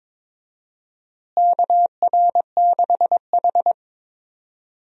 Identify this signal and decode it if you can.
Morse code, heard as a keyed tone, about 22 wpm, 700 Hz. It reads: KR6H
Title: USA Call Signs - 4 Characters - Repeated 2x - 22wpm